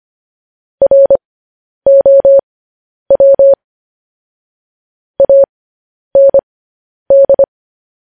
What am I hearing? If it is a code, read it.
Morse code, heard as a keyed tone, about 25 wpm, spaced as for 10 wpm, 550 Hz. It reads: ROW AND
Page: news-headlines-in-morse-code-at-10-wpm